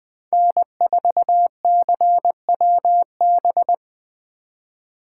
Morse code, heard as a keyed tone, about 20 wpm, 700 Hz. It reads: N4CWB
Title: 2023 POTA Activator Call Signs - 20wpm